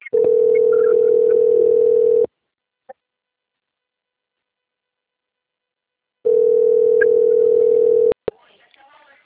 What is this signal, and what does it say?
Ringback tone (North America), 440 + 480 Hz, with speech.
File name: out- - - - - .1660